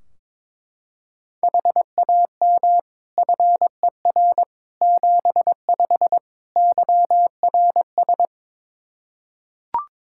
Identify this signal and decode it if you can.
Morse code, heard as a keyed tone, about 22 wpm, 700 Hz. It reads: HAM FER 75 YRS
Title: K1USN SST QSO Practice 22wpm